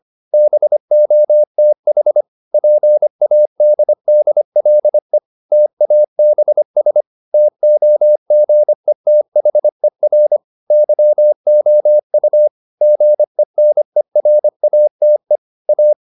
Morse code, heard as a keyed tone, about 25 wpm, 600 Hz. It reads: BOTH PADDLE TABS TOGETHER YOU GENERATE A